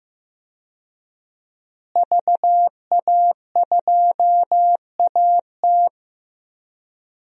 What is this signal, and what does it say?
Morse code, heard as a keyed tone, about 15 wpm, 700 Hz. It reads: VA2AT
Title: DXCC-100 Call Signs - 5 Characters - Repeated 2x - 15wpm